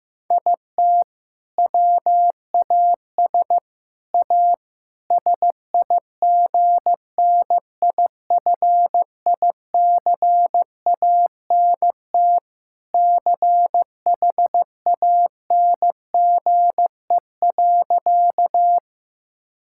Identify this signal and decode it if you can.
Morse code, heard as a keyed tone, about 15 wpm, 700 Hz. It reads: IT WAS A SIGNIFICANT CHANGE.
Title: Sentences from Top 1000 Words - Review Part 6 - 15wpm